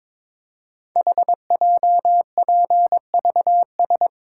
Morse code, heard as a keyed tone, about 22 wpm, 700 Hz. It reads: HJPVS
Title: Code Groups - 5 Letters - 22wpm